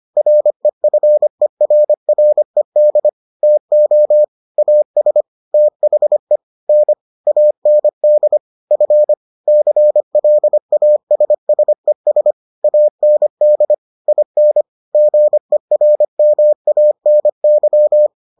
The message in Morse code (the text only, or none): REFERRED TO AS THE N AND F CLASSES AND IN GERMANY